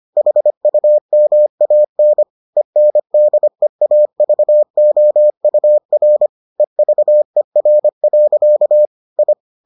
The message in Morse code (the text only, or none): HUMAN ENDEAVOUR EVER. I